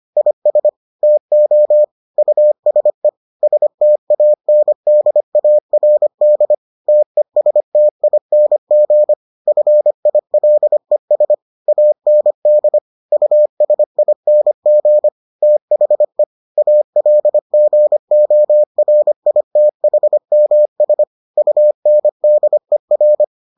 Morse code, heard as a keyed tone, about 25 wpm, 600 Hz. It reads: IS TO USE STANDARD TESTING FILES AND USING THE ALGORITHMS UNDER